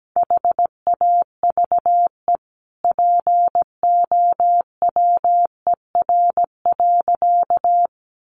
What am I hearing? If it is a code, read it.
Morse code, heard as a keyed tone, about 17 wpm, 700 Hz. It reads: HAVE POWER.
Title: Sentences from Top 1100 Words 17wpm